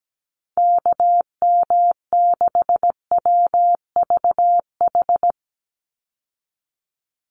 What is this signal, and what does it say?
Morse code, heard as a keyed tone, about 17 wpm, 700 Hz. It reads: KM6WVH
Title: USA Call Signs 17wpm